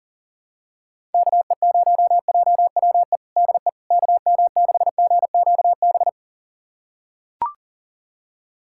Morse code, heard as a keyed tone, about 37 wpm, 700 Hz. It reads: KE0JWE DE KM6GQB
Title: USA Callsign DE Callsign 40wpm